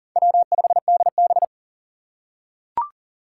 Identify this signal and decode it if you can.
Morse code, heard as a keyed tone, about 42 wpm, 700 Hz. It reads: W5DB
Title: USA Call Signs - 4 Characters - 4x Character Spacing - 40wpm